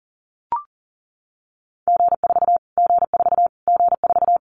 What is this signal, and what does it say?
Morse code, heard as a keyed tone, about 40 wpm, 700 Hz. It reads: G4 G4 G4
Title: Callsign Prefixes for Top 25 DXCC - Repeated 3x - 40wpm